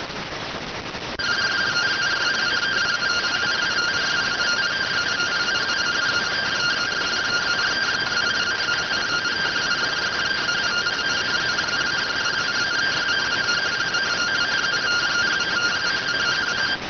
Two-tone teleprinter signal, about 170 Hz shift.